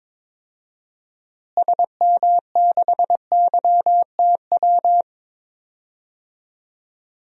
Morse code, heard as a keyed tone, about 22 wpm, 700 Hz. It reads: SM6YTW
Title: DXCC-100 Call Signs - 6 Characters - Repeated 2x - 22wpm